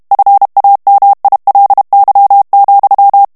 Morse code, heard as a keyed tone, about 32 wpm, 800 Hz. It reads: FAMILY,
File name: morse mp3 / ANGLAIS texte clair 1 vitesse 30 mots minute